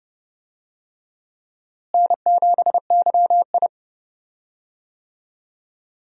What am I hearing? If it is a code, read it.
Morse code, heard as a keyed tone, about 30 wpm, 700 Hz. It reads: N7YI
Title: USA Call Signs - 4 Characters - Repeated 2x - 30wpm